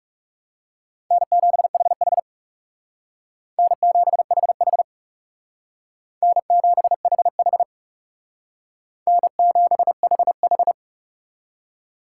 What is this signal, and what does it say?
Morse code, heard as a keyed tone, about 39 wpm, 700 Hz. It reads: N7HH N7HH N7HH N7HH